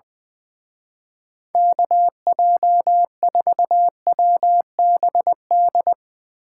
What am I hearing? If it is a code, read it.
Morse code, heard as a keyed tone, about 22 wpm, 700 Hz. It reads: KJ4WBD
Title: Speed Racing - USA 2x3 Call Signs 20wpm